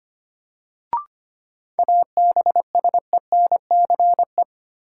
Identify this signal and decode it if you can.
Morse code, heard as a keyed tone, about 25 wpm, 700 Hz. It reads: ABSENCE
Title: Sets of 4 Words - New Word Drill - 25wpm